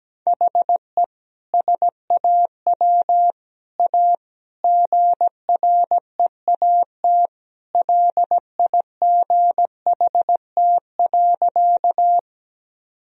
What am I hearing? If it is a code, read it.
Morse code, heard as a keyed tone, about 17 wpm, 700 Hz. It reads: HE SAW A GREAT LIGHT.
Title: Sentences from Top 1000 Words - Review Part 1 - 17wpm